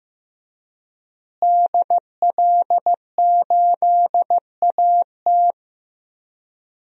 Morse code, heard as a keyed tone, about 15 wpm, 700 Hz. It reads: DL8AT
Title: DXCC-100 Call Signs - 5 Characters - Repeated 3x - 15wpm